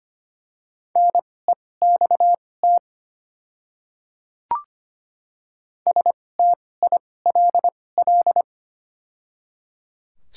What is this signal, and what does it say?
Morse code, heard as a keyed tone, about 25 wpm, spaced as for 18 wpm, 700 Hz. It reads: NEXT STILL
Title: 2x Character Spacing - Top 300 Words - 25wpm